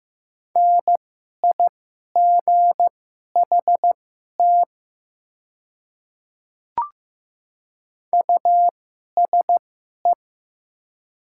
Morse code, heard as a keyed tone, about 15 wpm, spaced as for 11 wpm, 700 Hz. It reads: NIGHT USE
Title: 2x Character Spacing - Top 300 Words - 15wpm